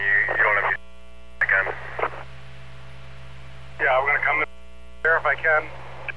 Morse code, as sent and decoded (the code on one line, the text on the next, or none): none